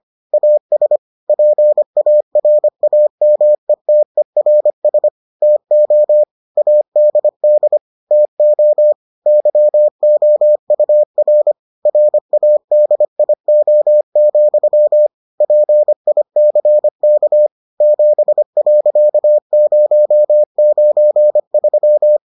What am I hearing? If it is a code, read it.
Morse code, heard as a keyed tone, about 25 wpm, 600 Hz. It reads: AS PARAMETERS TO ADD TO YOUR RADIO, PICK 7.093